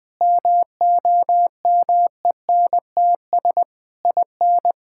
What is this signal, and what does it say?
Morse code, heard as a keyed tone, about 21 wpm, 700 Hz. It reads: MOMENTS IN
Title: Difficult Sentences with Hints - 20wpm